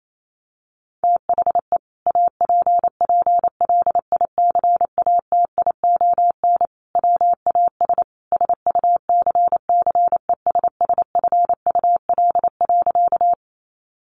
Morse code, heard as a keyed tone, about 28 wpm, 700 Hz. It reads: THE APPLICATION WAS SUCCESSFUL.
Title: Sentences from Top 1200 Words 28wpm